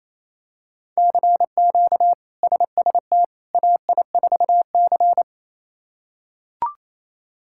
Morse code, heard as a keyed tone, about 28 wpm, 700 Hz. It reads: CQ SST AI4C
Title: K1USN SST QSO Practice - 28wpm